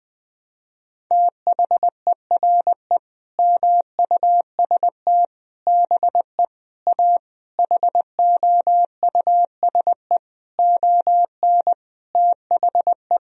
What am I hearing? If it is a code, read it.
Morse code, heard as a keyed tone, about 20 wpm, 700 Hz. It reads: THERE MUST BE A HOUSE ON THE
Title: Sentences from Top 1000 Words - Review Part 1 - 20wpm